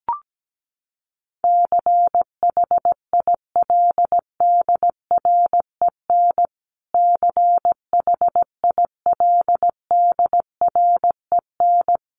Morse code, heard as a keyed tone, about 17 wpm, 700 Hz. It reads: CHILDREN CHILDREN
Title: Top 300 Words Repeated 17wpm